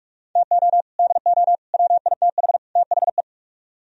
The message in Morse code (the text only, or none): TO DO WITH THE